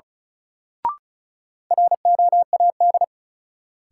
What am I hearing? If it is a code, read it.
Morse code, heard as a keyed tone, about 36 wpm, 700 Hz. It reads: ROAD
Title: Sentences from Top 400 Words - New Word Drill - 35wpm